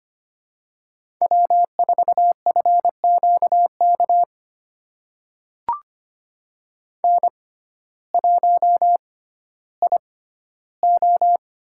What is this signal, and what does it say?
Morse code, heard as a keyed tone, about 25 wpm, spaced as for 14 wpm, 700 Hz. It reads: W4FQK N1IO
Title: USA Call Signs - 6x Character Spacing - 25wpm